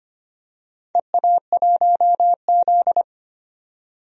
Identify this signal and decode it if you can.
Morse code, heard as a keyed tone, about 25 wpm, 700 Hz. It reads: EA1Z
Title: DXCC-100 Call Signs - 4 Characters - 25wpm